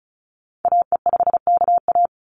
Morse code, heard as a keyed tone, about 34 wpm, 700 Hz. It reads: AE5KA
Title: NA SOTA Activator Callsigns 35wpm